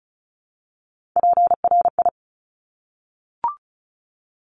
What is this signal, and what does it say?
Morse code, heard as a keyed tone, about 36 wpm, 700 Hz. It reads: PRI
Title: Top Three Letter Combinations in Words 35wpm